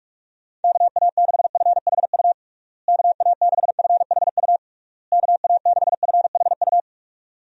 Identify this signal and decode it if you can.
Morse code, heard as a keyed tone, about 47 wpm, 700 Hz. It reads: KA6FHU KA6FHU KA6FHU
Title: USA Call Signs - 6 Characters - Repeated 3x - 45wpm